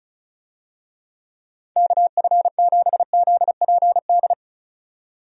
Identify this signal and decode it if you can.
Morse code, heard as a keyed tone, about 34 wpm, 700 Hz. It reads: KF7ZPD
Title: USA Call Signs - 6 Characters - 35wpm